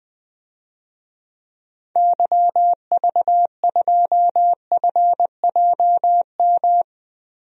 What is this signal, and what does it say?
Morse code, heard as a keyed tone, about 20 wpm, 700 Hz. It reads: YV2FJM
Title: DXCC-100 Call Signs - 6 Characters - Repeated 2x - 20wpm